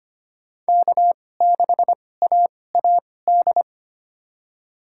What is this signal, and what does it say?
Morse code, heard as a keyed tone, about 25 wpm, spaced as for 18 wpm, 700 Hz. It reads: K6AAD
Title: USA Call Signs - 5 Characters - 2x Character Spacing - 25wpm